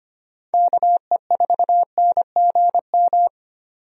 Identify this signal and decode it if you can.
Morse code, heard as a keyed tone, about 25 wpm, 700 Hz. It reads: KE4NGM
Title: USA Call Signs - 6 Characters - 25wpm